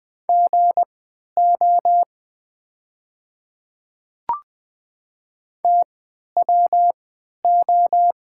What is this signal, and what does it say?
Morse code, heard as a keyed tone, about 20 wpm, spaced as for 11 wpm, 700 Hz. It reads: GO TWO
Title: 3x Character Spacing - Top 200 Words - 20wpm